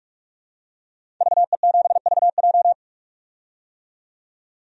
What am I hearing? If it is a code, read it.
Morse code, heard as a keyed tone, about 46 wpm, 700 Hz. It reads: VE7VJ
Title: DXCC-100 Call Signs - 5 Characters - 45wpm